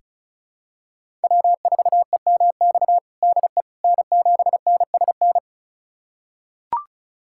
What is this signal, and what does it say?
Morse code, heard as a keyed tone, about 36 wpm, 700 Hz. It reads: W4EMX DE N7NSN
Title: USA Callsign DE Callsign - 5 Characters - 35wpm